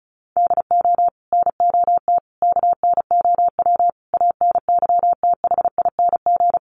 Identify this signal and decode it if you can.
Morse code, heard as a keyed tone, about 34 wpm, 700 Hz. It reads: DO NOT KNOW ANYTHING
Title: Sets of 5 Words 35wpm